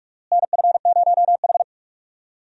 Morse code, heard as a keyed tone, about 46 wpm, 700 Hz. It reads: NF0H
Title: USA Call Signs - 4 Characters - Repeated 2x - 45wpm